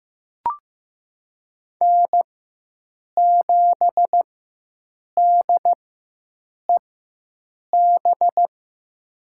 Morse code, heard as a keyed tone, about 15 wpm, spaced as for 7 wpm, 700 Hz. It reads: N7DEB
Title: USA Call Signs - 5 Characters - 4x Character Spacing - 15wpm